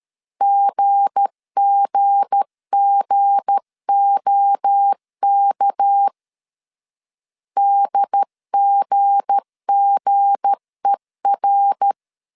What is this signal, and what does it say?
Morse code, heard as a keyed tone, about 13 wpm, 800 Hz. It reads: GGGOK DGGER
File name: LEZIONI / 7 GF9Z / 7 G 12wpm 2055